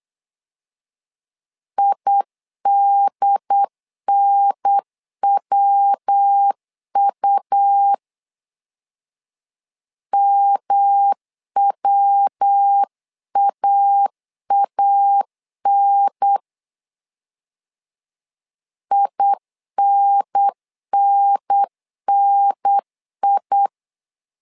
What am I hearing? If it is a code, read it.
Morse code, heard as a keyed tone, about 8 wpm, 800 Hz. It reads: IDNWU MWAAN INNNI
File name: LEZIONI / 3 MWRO / 3 MW 8wpm 2501